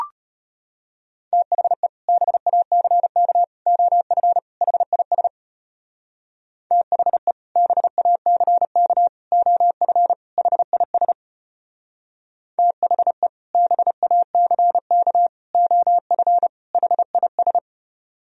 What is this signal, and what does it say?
Morse code, heard as a keyed tone, about 32 wpm, 700 Hz. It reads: THE BACK OF HIS THE BACK OF HIS THE BACK OF HIS